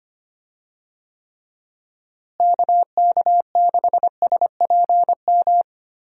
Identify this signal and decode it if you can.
Morse code, heard as a keyed tone, about 25 wpm, 700 Hz. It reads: KK6SPM
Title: USA Call Signs - 6 Characters - Repeated 2x - 25wpm